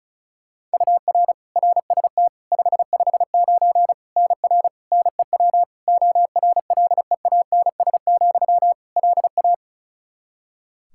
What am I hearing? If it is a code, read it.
Morse code, heard as a keyed tone, about 34 wpm, 700 Hz. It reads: UR RST 559 NR NEW ORLEANS, LA
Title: K1USN SST QSO Practice 35wpm